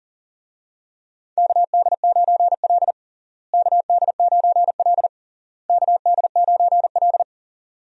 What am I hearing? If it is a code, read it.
Morse code, heard as a keyed tone, about 40 wpm, 700 Hz. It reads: KD9L KD9L KD9L
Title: USA Call Signs - 4 Characters - Repeated 3x - 40wpm